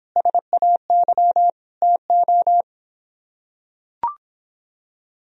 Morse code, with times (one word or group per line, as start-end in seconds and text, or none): 0.2-1.5 SAY
1.8-2.6 TO